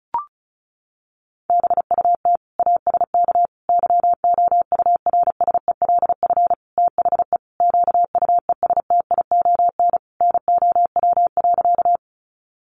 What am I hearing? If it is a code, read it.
Morse code, heard as a keyed tone, about 34 wpm, 700 Hz. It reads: BUT ASK YOURSELF THE QUESTION NOW.
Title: Sentences from Top 600 Words 35wpm